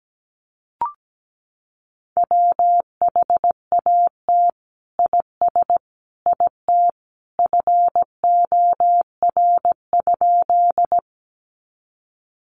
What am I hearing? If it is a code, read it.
Morse code, heard as a keyed tone, about 17 wpm, 700 Hz. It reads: WHAT IS IT FOR?
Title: Sentences from Top 100 Words 17wpm